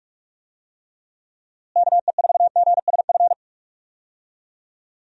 Morse code, heard as a keyed tone, about 47 wpm, 700 Hz. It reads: KE4GSF
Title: USA Call Signs - 6 Characters - Repeated 2x - 45wpm